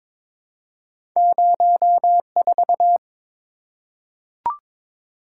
Morse code, heard as a keyed tone, about 22 wpm, 690 Hz. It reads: 04
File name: practice / 22wpm / Two Letter-Number Combinations - 22wpm